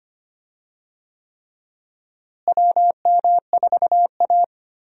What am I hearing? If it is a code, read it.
Morse code, heard as a keyed tone, about 25 wpm, 700 Hz. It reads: WM4A
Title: USA Call Signs - 4 Characters - Repeated 3x - 25wpm